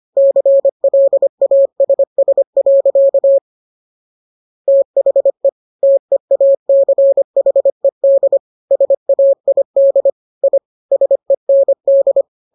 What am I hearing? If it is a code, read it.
Morse code, heard as a keyed tone, about 25 wpm, 550 Hz. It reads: CLASS. THE TEACHED SAID I SEND